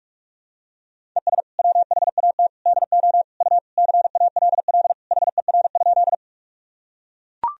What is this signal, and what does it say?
Morse code, heard as a keyed tone, about 42 wpm, 700 Hz. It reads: ES WHAT DO U CALL HER?
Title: Sentences from Top 100 Words and Abbreviations 45wpm